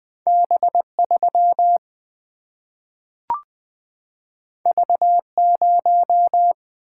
Morse code, heard as a keyed tone, about 20 wpm, 700 Hz. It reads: B3 V0
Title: Single-Letter Single-Number 20wpm